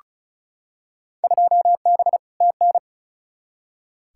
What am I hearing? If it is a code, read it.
Morse code, heard as a keyed tone, about 36 wpm, 700 Hz. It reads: 2B TN
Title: Field Day Exchanges Encore - 35wpm